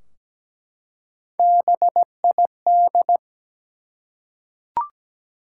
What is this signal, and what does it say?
Morse code, heard as a keyed tone, about 17 wpm, 700 Hz. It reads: BID
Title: Three Letter Words - 17wpm